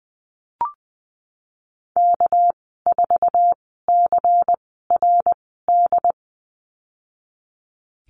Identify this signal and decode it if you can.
Morse code, heard as a keyed tone, about 20 wpm, spaced as for 15 wpm, 700 Hz. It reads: K4CRD
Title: 231 Lesson 37g - Letter x - Call Signs 20wpm